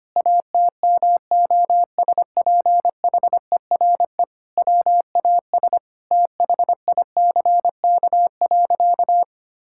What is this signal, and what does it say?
Morse code, heard as a keyed tone, about 25 wpm, 700 Hz. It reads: ATMOSPHERE WAS THICK.
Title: Sentences from Top 1600 Words 25wpm